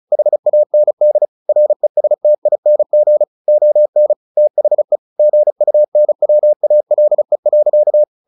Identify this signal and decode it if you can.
Morse code, heard as a keyed tone, about 34 wpm, 600 Hz. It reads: HAND RESTING ON THE GUNWALE.
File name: the old man and the sea 2 35wpm